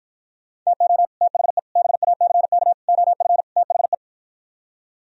Morse code, heard as a keyed tone, about 52 wpm, 700 Hz. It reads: TO THE BACK OF THE